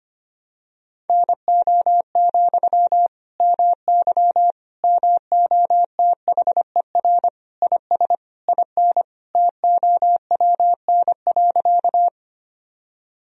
Morse code, heard as a keyed tone, about 25 wpm, 700 Hz. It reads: NO, MY MOTHER IS IN TOWN.
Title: Sentences from Top 1000 Words - Review Part 2 - 25wpm